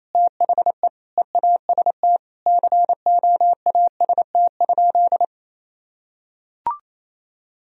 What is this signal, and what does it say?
Morse code, heard as a keyed tone, about 28 wpm, 700 Hz. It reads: THE EAST COAST?
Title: Sentences from Top 1000 Words - Review Part 4 - 28wpm